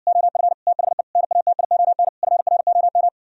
Morse code, heard as a keyed tone, about 59 wpm, 700 Hz. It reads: OF THE NATION FROM